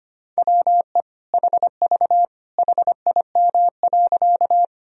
Morse code, heard as a keyed tone, about 25 wpm, 700 Hz. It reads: WE HV HIM.
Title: Sentences from Top 100 Words and Abbreviations 25wpm